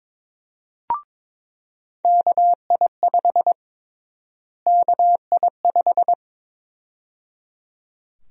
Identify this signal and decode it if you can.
Morse code, heard as a keyed tone, about 22 wpm, 700 Hz. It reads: KI5 KI5
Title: USA Call Sign Prefixes - Repeated 2x - 22wpm